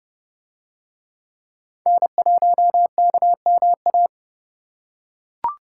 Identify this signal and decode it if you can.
Morse code, heard as a keyed tone, about 28 wpm, 700 Hz. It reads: N1KMA